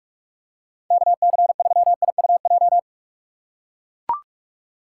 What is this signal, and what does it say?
Morse code, heard as a keyed tone, about 42 wpm, 700 Hz. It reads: KC3IUJ
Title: USA Call Signs - 6 Characters - 6x Character Spacing - 45wpm